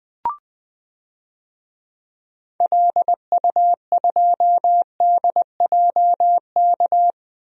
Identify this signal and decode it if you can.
Morse code, heard as a keyed tone, about 20 wpm, 700 Hz. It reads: LU2DJK
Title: DXCC-100 Call Signs - 6 Characters - Repeated 3x - 20wpm